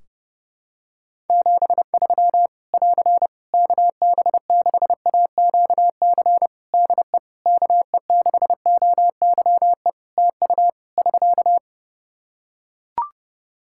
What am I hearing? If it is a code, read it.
Morse code, heard as a keyed tone, about 30 wpm, 700 Hz. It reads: 73 <AR> KB6AQC DE KE6OYE TU <SK>
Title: Practice QSO - Exchange 4 - Ending - 30wpm